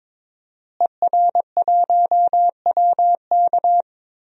Morse code, heard as a keyed tone, about 22 wpm, 700 Hz. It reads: ER1WK
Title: DXCC-100 Call Signs - 5 Characters - Repeated 3x - 22wpm